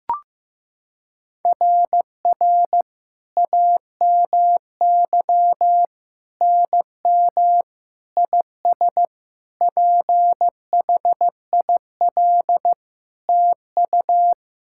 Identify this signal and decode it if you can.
Morse code, heard as a keyed tone, about 15 wpm, 700 Hz. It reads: RR AMY NM IS PHIL TU
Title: SOTA QSO Practice 15wpm